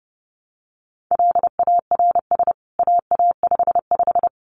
Morse code, heard as a keyed tone, about 30 wpm, 700 Hz. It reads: LARS AA55
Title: CWT Exchange 30wpm